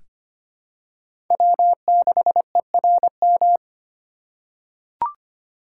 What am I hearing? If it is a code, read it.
Morse code, heard as a keyed tone, about 25 wpm, 700 Hz. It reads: W6ERM
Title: USA Call Signs - 5 Characters - 6x Character Spacing - 25wpm